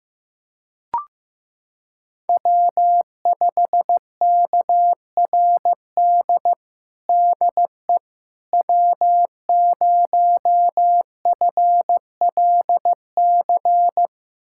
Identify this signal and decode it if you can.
Morse code, heard as a keyed tone, about 16 wpm, 700 Hz. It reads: W5KRD DE W0FLC
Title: USA Callsign DE Callsign - 5 Characters - 15wpm